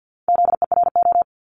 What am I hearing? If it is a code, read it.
Morse code, heard as a keyed tone, about 50 wpm, 700 Hz. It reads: ZERO
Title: Multi-letter Phonemes 50wpm